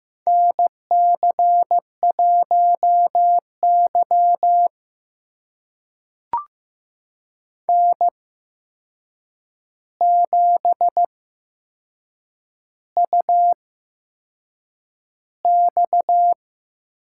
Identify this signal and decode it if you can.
Morse code, heard as a keyed tone, about 15 wpm, spaced as for 6 wpm, 700 Hz. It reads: NC1Y N7UX
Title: USA Call Signs - 4 Characters - 8x Character Spacing - 15wpm